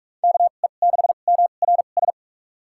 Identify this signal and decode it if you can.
Morse code, heard as a keyed tone, about 42 wpm, spaced as for 31 wpm, 700 Hz. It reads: KE6MRS
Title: USA Call Signs - 6 Characters - 2x Character Spacing - 45wpm